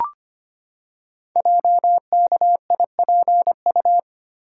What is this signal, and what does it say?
Morse code, heard as a keyed tone, about 25 wpm, 700 Hz. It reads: JKIPU
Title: Code Groups - 5 Letters - 25wpm